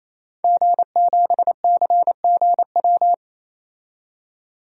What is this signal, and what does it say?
Morse code, heard as a keyed tone, about 28 wpm, 700 Hz. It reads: G7CGW
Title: DXCC-100 Call Signs - 5 Characters - 28wpm